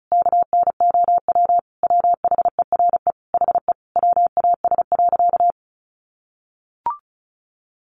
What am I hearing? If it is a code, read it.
Morse code, heard as a keyed tone, about 34 wpm, 700 Hz. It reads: KNOW WHERE HE WAS.
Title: Sentences from Top 600 Words 35wpm